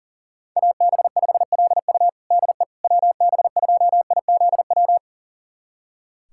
Morse code, heard as a keyed tone, about 40 wpm, 700 Hz. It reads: AB5LU DE WB2IZW
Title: USA Callsign DE Callsign 40wpm